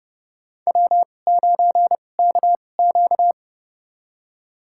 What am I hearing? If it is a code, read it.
Morse code, heard as a keyed tone, about 31 wpm, spaced as for 22 wpm, 700 Hz. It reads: W9KQ
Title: USA Call Signs - 4 Characters - 2x Character Spacing - 30wpm